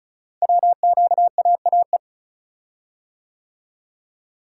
Morse code, heard as a keyed tone, about 36 wpm, 700 Hz. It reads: WQAAE
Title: Code Groups - 5 Letter:Numbers - 35wpm